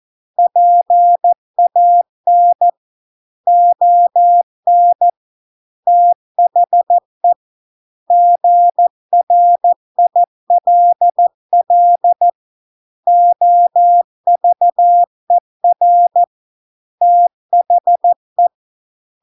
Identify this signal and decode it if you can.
Morse code, heard as a keyed tone, about 14 wpm, 700 Hz. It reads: PAN ON THE GRILL OVER THE